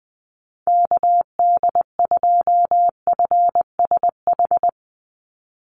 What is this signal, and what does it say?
Morse code, heard as a keyed tone, about 20 wpm, 700 Hz. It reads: KD2FSH
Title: Call Signs Repeated 3x - 20wpm